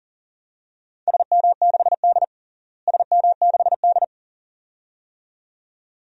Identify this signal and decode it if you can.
Morse code, heard as a keyed tone, about 40 wpm, 700 Hz. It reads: SM6D SM6D
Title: DXCC-100 Call Signs - 4 Characters - Repeated 2x - 40wpm